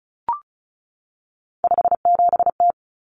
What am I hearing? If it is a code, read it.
Morse code, heard as a keyed tone, about 34 wpm, 700 Hz. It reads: 57T
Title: Common Cut Numbers - 3-Digit 35wpm